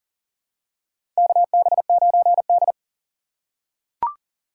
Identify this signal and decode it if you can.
Morse code, heard as a keyed tone, about 40 wpm, 700 Hz. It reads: KB9D
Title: USA Call Signs - 4x Character Spacing - 40wpm